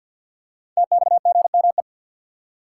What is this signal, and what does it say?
Morse code, heard as a keyed tone, about 50 wpm, 700 Hz. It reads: TXGME
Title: Code Groups - 5 Letter:Numbers - 50wpm